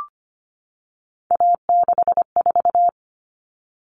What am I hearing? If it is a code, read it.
Morse code, heard as a keyed tone, about 25 wpm, 700 Hz. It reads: A64
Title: All Cut Numbers - 3-Digit 25wpm